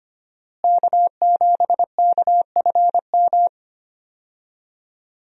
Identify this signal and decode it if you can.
Morse code, heard as a keyed tone, about 25 wpm, 700 Hz. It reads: K7KFM
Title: USA Call Signs - 5 Characters - Repeated 2x - 25wpm